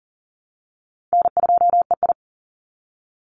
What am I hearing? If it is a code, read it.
Morse code, heard as a keyed tone, about 42 wpm, 700 Hz. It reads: N2EI
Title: Easy Call Signs - 40wpm